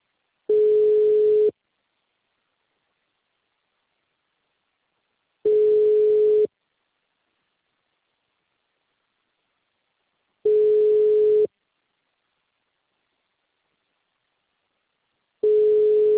Ringback tone (Europe, 425 Hz).
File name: out- -2041- - - .33